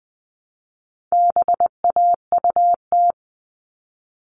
Morse code, heard as a keyed tone, about 20 wpm, 700 Hz. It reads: BAUT
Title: All Cut Numbers - 4-Digit 20wpm